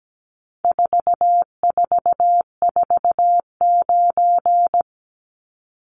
Morse code, heard as a keyed tone, about 17 wpm, 700 Hz. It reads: 4449
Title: Numbers - 4-Digit 17wpm